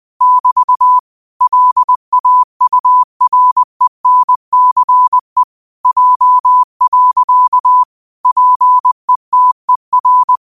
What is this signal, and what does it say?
Morse code, heard as a keyed tone, about 20 wpm, 1000 Hz. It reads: <BT> LAURENCE J. PETER
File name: qotdfiles / 20wpm